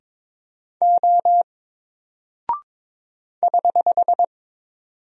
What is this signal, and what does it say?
Morse code, heard as a keyed tone, about 22 wpm, 700 Hz. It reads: O <HH>
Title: Single Character (Letter:Number:Punctuation:Prosign) 22wpm